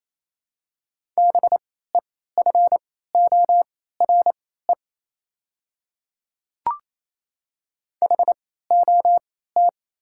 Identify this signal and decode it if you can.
Morse code, heard as a keyed tone, about 27 wpm, spaced as for 16 wpm, 700 Hz. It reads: BEFORE HOT
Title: 3x Character Spacing - Top 200 Words - 28wpm